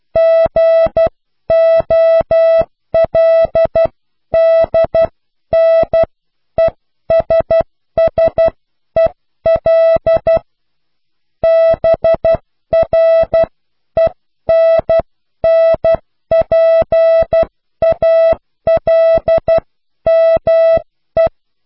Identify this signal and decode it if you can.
Morse code, heard as a keyed tone, about 12 wpm, spaced as for 10 wpm, 660 Hz. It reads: GOLDNESSEL BRENNPALME
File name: cw / pseudo worden(10)